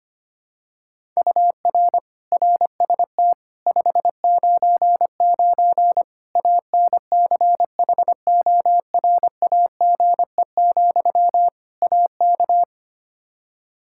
Morse code, heard as a keyed tone, about 25 wpm, 700 Hz. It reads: UR RST 599 ANCHORAGE, AK
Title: K1USN SST QSO Practice 25wpm